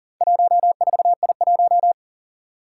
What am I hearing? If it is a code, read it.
Morse code, heard as a keyed tone, about 40 wpm, 700 Hz. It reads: 14I1
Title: Four Challenging Letter-Number Combinations 40wpm